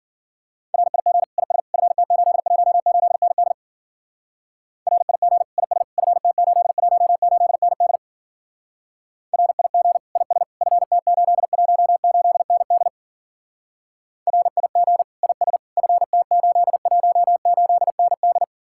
Morse code, heard as a keyed tone, about 49 wpm, 700 Hz. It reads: RIG IS FT818ND RIG IS FT818ND RIG IS FT818ND RIG IS FT818ND
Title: Speed-Racing Common QSO Phrases - 40wpm